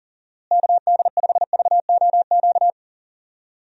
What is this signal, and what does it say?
Morse code, heard as a keyed tone, about 40 wpm, 700 Hz. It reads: KD5VOQ
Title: USA Call Signs - 6 Characters - Repeated 2x - 40wpm